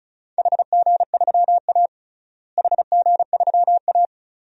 Morse code, heard as a keyed tone, about 34 wpm, 700 Hz. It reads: HG3A HG3A
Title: DXCC-100 Call Signs - 4 Characters - Repeated 2x - 35wpm